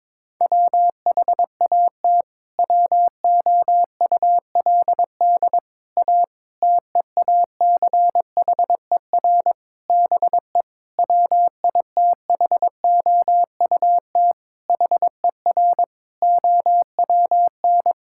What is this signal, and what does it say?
Morse code, heard as a keyed tone, about 22 wpm, 700 Hz. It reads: WHAT WOULD A TEACHER BE WITHOUT HER OWN
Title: Sentences from Top 1000 Words - Review Part 6 - 22wpm